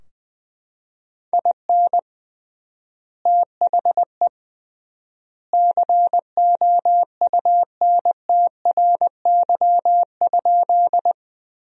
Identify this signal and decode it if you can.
Morse code, heard as a keyed tone, about 20 wpm, 700 Hz. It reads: IN THE COUNTRY?
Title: 3x Word-Spacing - Sentences from Top 300 Words - 20wpm